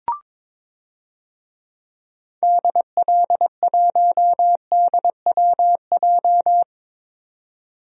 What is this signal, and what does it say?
Morse code, heard as a keyed tone, about 22 wpm, 700 Hz. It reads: DL1DWJ
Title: DXCC-100 Call Signs - 6 Characters - 22wpm